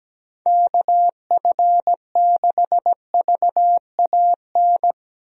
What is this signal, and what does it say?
Morse code, heard as a keyed tone, about 17 wpm, 700 Hz. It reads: KF6VAN
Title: USA Call Signs - 6 Characters - 17wpm